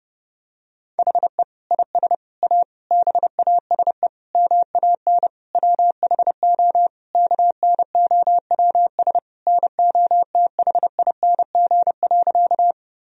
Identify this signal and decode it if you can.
Morse code, heard as a keyed tone, about 30 wpm, 700 Hz. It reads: HE IS A BASE MAN WHO KNOWS NOTHING.
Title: Sentences from Top 1000 Words - Review Part 1 - 30wpm